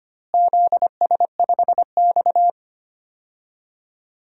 Morse code, heard as a keyed tone, about 25 wpm, 700 Hz. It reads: ZS5X
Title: DXCC-100 Call Signs - 4 Characters - Repeated 3x - 25wpm